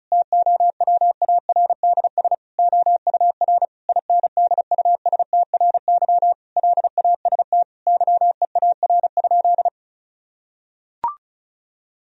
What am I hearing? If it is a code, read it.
Morse code, heard as a keyed tone, about 34 wpm, 700 Hz. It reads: TOWARDS OUR INDUSTRY LAST YEAR?
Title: Sentences from Top 1000 Words - Review Part 4 - 35wpm